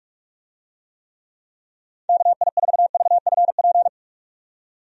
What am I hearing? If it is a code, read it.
Morse code, heard as a keyed tone, about 48 wpm, 700 Hz. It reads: KI4VFP
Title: USA Call Signs - 6 Characters - 45wpm